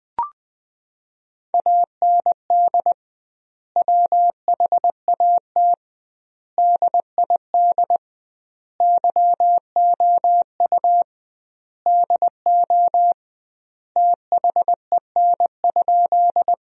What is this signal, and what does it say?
Morse code, heard as a keyed tone, about 20 wpm, 700 Hz. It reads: AND WHAT DID YOU DO THEN?
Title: 2x Word-Spacing - Sentences from Top 100 Words - 20wpm